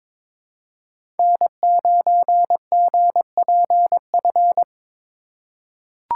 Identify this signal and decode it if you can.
Morse code, heard as a keyed tone, about 22 wpm, 700 Hz. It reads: N9GPF
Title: USA Call Signs - 5 Characters - 2x Character Spacing - 22wpm